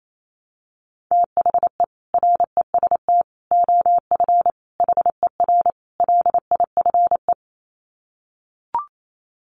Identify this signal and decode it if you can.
Morse code, heard as a keyed tone, about 28 wpm, 700 Hz. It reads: THE REST OF HER LIFE
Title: Sets of 5 Words Encore 28wpm